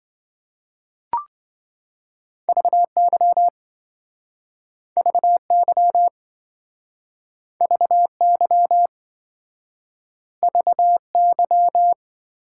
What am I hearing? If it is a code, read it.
Morse code, heard as a keyed tone, about 27 wpm, 700 Hz. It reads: VY VY VY VY